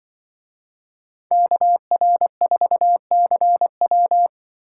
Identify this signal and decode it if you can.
Morse code, heard as a keyed tone, about 24 wpm, 700 Hz. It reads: KR4CW
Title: Speed Racing - USA Call Signs 20wpm